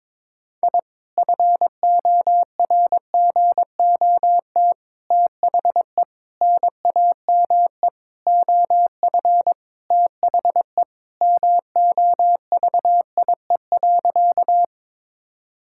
Morse code, heard as a keyed tone, about 21 wpm, 700 Hz. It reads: I FORGOT THE NAME OF THE MOVIE.
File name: practice / 22wpm / Sentences with 4-Word Phrases - 22wpm